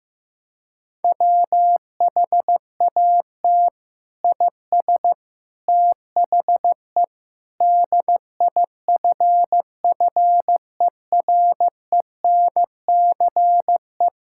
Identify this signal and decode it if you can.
Morse code, heard as a keyed tone, about 15 wpm, 700 Hz. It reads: WHAT IS THE DIFFERENCE
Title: Sentences from Top 1000 Words - Review Part 5 - 15wpm